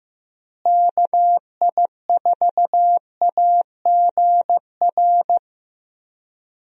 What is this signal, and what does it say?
Morse code, heard as a keyed tone, about 15 wpm, 700 Hz. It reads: KI4AGR
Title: USA Call Signs - 6 Characters - Repeated 2x - 15wpm